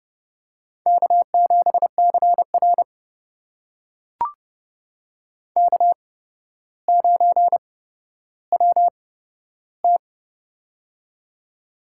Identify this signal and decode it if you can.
Morse code, heard as a keyed tone, about 30 wpm, spaced as for 13 wpm, 700 Hz. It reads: K7CR K9WT
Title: USA Call Signs - 4 Characters - 8x Character Spacing - 30wpm